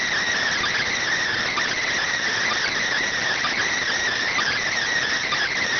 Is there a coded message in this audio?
Yes: Emergency Alert System (SAME) tones from 0.6 to 4.7 s, heard as harsh buzzing bursts.